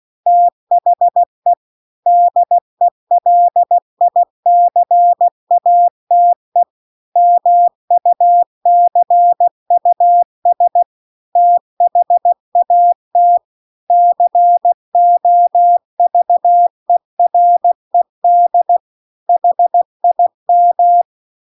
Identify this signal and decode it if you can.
Morse code, heard as a keyed tone, about 16 wpm, 700 Hz. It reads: THE DELICATE MUCUS THAT COVERED HIM